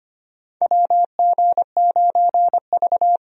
Morse code, heard as a keyed tone, about 25 wpm, 700 Hz. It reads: WG9V
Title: USA Call Signs - 4 Characters - 25wpm